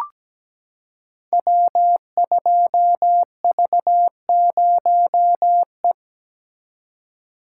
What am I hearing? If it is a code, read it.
Morse code, heard as a keyed tone, about 17 wpm, 700 Hz. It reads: W2V0E
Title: Code Groups - 5 Letter:Numbers - 17wpm